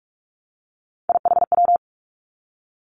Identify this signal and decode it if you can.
Morse code, heard as a keyed tone, about 46 wpm, 700 Hz. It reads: IHW
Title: Three Challenging Letter-Number Combinations 45wpm